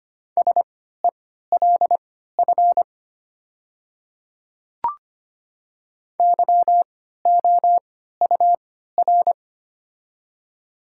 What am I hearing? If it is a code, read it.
Morse code, heard as a keyed tone, about 25 wpm, spaced as for 14 wpm, 700 Hz. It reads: SELF YOUR
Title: 3x Character Spacing - Top 300 Words - 25wpm